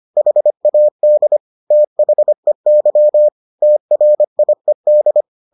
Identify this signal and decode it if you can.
Morse code, heard as a keyed tone, about 25 wpm, 600 Hz. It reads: HAD THEY TRIED